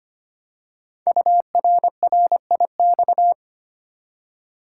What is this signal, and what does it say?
Morse code, heard as a keyed tone, about 25 wpm, 700 Hz. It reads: URRIX
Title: Code Groups - 5 Letters - 25wpm